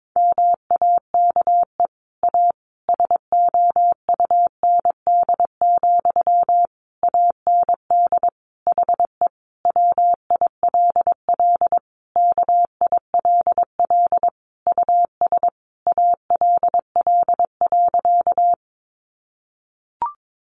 Morse code, heard as a keyed tone, about 22 wpm, 700 Hz. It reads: MAKE A SOUND, AND HE WILL KILL US ALL.
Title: Sentences from Top 1400 Words 22wpm